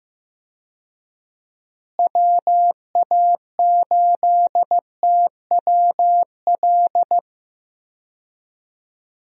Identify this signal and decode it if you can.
Morse code, heard as a keyed tone, about 16 wpm, 700 Hz. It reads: WA8TWL
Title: USA Call Signs - 6 Characters - Repeated 3x - 15wpm